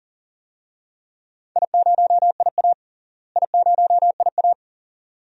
Morse code, heard as a keyed tone, about 40 wpm, 700 Hz. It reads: I0IA I0IA
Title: DXCC-100 Call Signs - 4 Characters - Repeated 2x - 40wpm